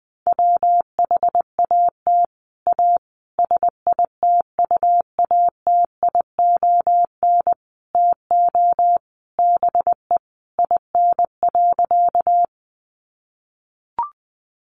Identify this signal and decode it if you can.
Morse code, heard as a keyed tone, about 20 wpm, 700 Hz. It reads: WHAT A SITUATION TO BE IN.
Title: Sentences from Top 1200 Words 20wpm